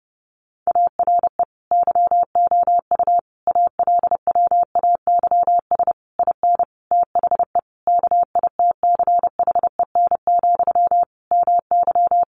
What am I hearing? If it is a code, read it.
Morse code, heard as a keyed tone, about 28 wpm, 700 Hz. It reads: ARE YOU ALWAYS IN THE KITCHEN, MY
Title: Sentences from Top 1400 Words 30wpm